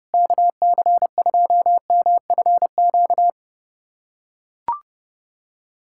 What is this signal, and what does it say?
Morse code, heard as a keyed tone, about 30 wpm, 700 Hz. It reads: KC2MFQ
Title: USA Callsign DE Callsign 30wpm